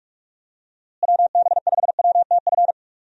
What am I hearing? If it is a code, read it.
Morse code, heard as a keyed tone, about 44 wpm, 700 Hz. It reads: WB5WTF
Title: USA Call Signs - 6 Characters - Repeated 2x - 45wpm